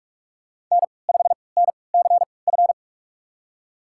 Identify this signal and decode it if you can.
Morse code, heard as a keyed tone, about 42 wpm, spaced as for 25 wpm, 700 Hz. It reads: N5NCF
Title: USA Call Signs - 5 Characters - 3x Character Spacing - 45wpm